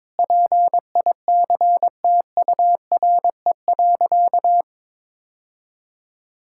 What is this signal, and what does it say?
Morse code, heard as a keyed tone, about 22 wpm, 700 Hz. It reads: PICTURE.
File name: practice / 22wpm / Sentences with 2-Word Phrases - 22wpm